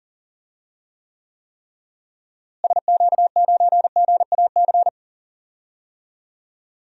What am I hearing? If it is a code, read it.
Morse code, heard as a keyed tone, about 40 wpm, 700 Hz. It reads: SQ9GAC
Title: DXCC-100 Call Signs - 6 Characters - Repeated 3x - 40wpm